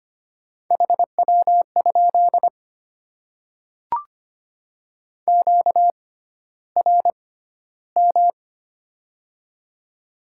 Morse code, heard as a keyed tone, about 25 wpm, spaced as for 13 wpm, 700 Hz. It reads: HW? QRM
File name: Top 90 QSO Elements - 6x Character Spacing - 25wpm